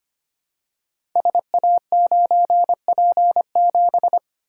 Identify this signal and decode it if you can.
Morse code, heard as a keyed tone, about 25 wpm, 700 Hz. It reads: SA9P7
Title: Code Groups - 5 Letter:Numbers - 25wpm